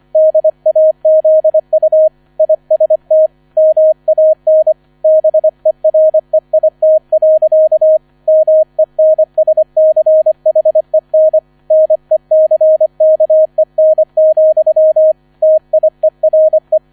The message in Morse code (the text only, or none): DAZU IST MAN BEREIT. MENSCHEN NECKEN, TIERE